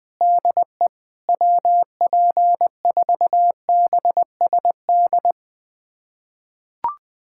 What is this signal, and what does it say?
Morse code, heard as a keyed tone, about 20 wpm, 700 Hz. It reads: DE WP4BSD
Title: USA Callsign DE Callsign 20wpm